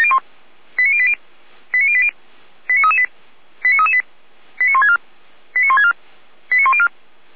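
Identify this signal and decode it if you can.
Five-tone selective call (ZVEI), heard as quick runs of five tones, tones of about 70 ms. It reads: 89080 89080 89338 89338 89164 89164 89104
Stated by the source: ZVEI five-tone call received by SDR, from the audio server of feuerwehr-orlamuende.org (fire brigade Orlamünde).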